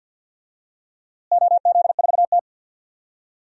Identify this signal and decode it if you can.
Morse code, heard as a keyed tone, about 51 wpm, 700 Hz. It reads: OZ4T
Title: DXCC-100 Call Signs - 4 Characters - Repeated 2x - 50wpm